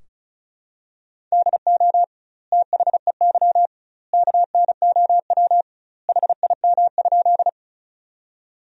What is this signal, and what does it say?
Morse code, heard as a keyed tone, about 36 wpm, 700 Hz. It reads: DO THEY KNOW HIM?
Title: 2x Word-Spacing - Sentences from Top 100 Words - 35wpm